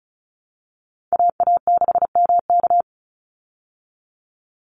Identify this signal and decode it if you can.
Morse code, heard as a keyed tone, about 34 wpm, 700 Hz. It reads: AA6MK
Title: Call Signs Repeated 3x - 35wpm